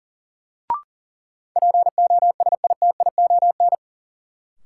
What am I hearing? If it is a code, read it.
Morse code, heard as a keyed tone, about 40 wpm, 700 Hz. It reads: POSITION
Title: Sentences from Top 900 Words - New Word Drill - 40wpm